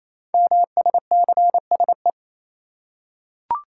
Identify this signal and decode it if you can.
Morse code, heard as a keyed tone, about 29 wpm, 700 Hz. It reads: MSCSE
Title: Code Groups - 5 Letter:Numbers - 28wpm